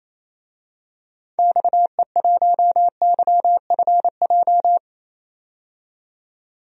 Morse code, heard as a keyed tone, about 28 wpm, 700 Hz. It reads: XE1YFJ
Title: DXCC-100 Call Signs - 6 Characters - 28wpm